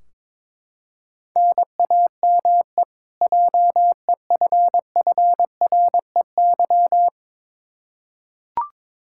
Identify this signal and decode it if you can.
Morse code, heard as a keyed tone, about 22 wpm, 700 Hz. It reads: NAME JEFFREY
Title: Practice QSO - Exchange 1 - Beginning - 22wpm